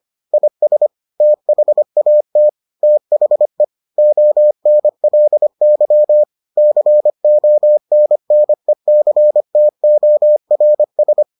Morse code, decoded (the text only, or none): IS THAT THE ONLY CONNECTORS